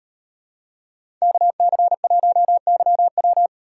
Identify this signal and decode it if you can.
Morse code, heard as a keyed tone, about 38 wpm, 700 Hz. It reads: KC1YW